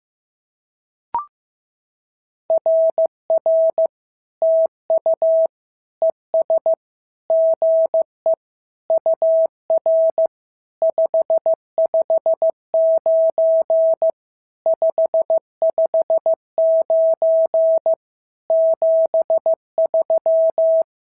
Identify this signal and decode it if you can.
Morse code, heard as a keyed tone, about 14 wpm, 650 Hz. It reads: RR TU ES GE UR 559 559 73
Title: SOTA QSO Practice 15wpm